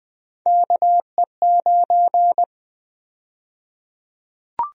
Morse code, heard as a keyed tone, about 20 wpm, 700 Hz. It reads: KE9
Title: USA Call Sign Prefixes - Repeated 2x - 20wpm